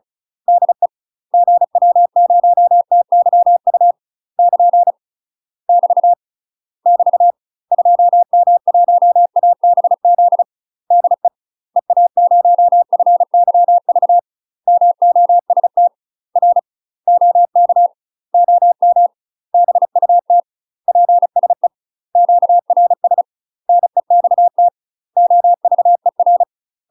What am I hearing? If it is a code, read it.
Morse code, heard as a keyed tone, about 34 wpm, 700 Hz. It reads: DE GW0TYU <KN> <BT> <BT> 2M1ABZ DE EA0FYV MOST R OK OM BUT PSE QRS NEXT OVER
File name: UK MASTER QSOs TRAINING / 35WPM / UK MASTER QSOs 0025